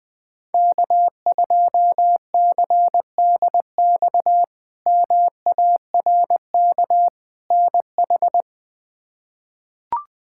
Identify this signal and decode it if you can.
Morse code, heard as a keyed tone, about 21 wpm, 700 Hz. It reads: K2CDX MARK NH
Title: K1USN SST QSO Practice - 20wpm